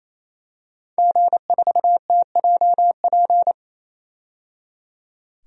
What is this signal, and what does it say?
Morse code, heard as a keyed tone, about 28 wpm, 700 Hz. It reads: G4TJP
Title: DXCC-100 Call Signs - 5 Characters - 28wpm